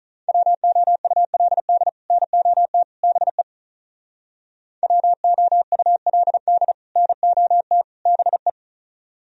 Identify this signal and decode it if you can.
Morse code, heard as a keyed tone, about 44 wpm, 700 Hz. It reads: WOULD NOT BE WOULD NOT BE
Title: Speed-Racing - Sets of 3 Words 35wpm